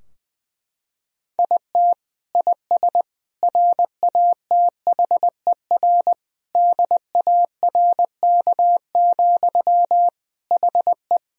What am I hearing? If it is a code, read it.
Morse code, heard as a keyed tone, about 20 wpm, 700 Hz. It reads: IT IS RATHER DARK, HE
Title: Sentences from Top 1000 Words - Review Part 4 - 20wpm